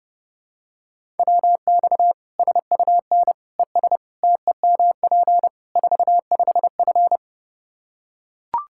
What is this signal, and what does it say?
Morse code, heard as a keyed tone, about 30 wpm, 700 Hz. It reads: WX SUN ES TEMP 45F
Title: Practice QSO - Exchange 2 - Info - 30wpm